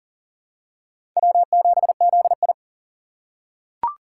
Morse code, heard as a keyed tone, about 40 wpm, 700 Hz. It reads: W7ZI
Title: USA Call Signs - 4 Characters - 3x Character Spacing - 40wpm